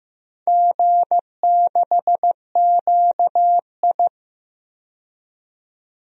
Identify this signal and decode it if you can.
Morse code, heard as a keyed tone, about 15 wpm, 700 Hz. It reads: G6QI
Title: DXCC-100 Call Signs - 4 Characters - 15wpm